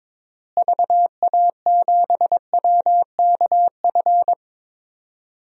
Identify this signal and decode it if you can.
Morse code, heard as a keyed tone, about 22 wpm, 700 Hz. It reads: VA7WKF
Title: DXCC-100 Call Signs - 6 Characters - Repeated 3x - 22wpm